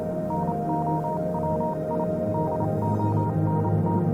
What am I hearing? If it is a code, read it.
Morse code, heard as a keyed tone, about 28 wpm, spaced as for 25 wpm, 940 Hz, with noise. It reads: NOW I DONT
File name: Vylet Pony / aura / do you remember the song she sang (morse code version)